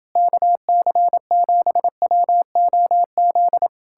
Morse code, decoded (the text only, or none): KC7WOZ